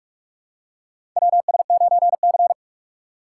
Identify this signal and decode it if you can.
Morse code, heard as a keyed tone, about 46 wpm, 700 Hz. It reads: WS9C
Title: USA Call Signs - 2x Character Spacing - 45wpm